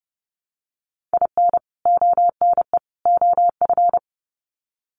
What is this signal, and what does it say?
Morse code, heard as a keyed tone, about 30 wpm, 700 Hz. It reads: IN ONE OF
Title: Sets of 3 Words 30wpm